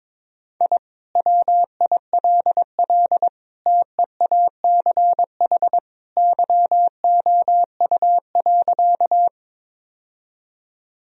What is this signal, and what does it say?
Morse code, heard as a keyed tone, about 22 wpm, 700 Hz. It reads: I WILL TEACH YOU.
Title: Sentences from Top 1000 Words - Review Part 2 - 22wpm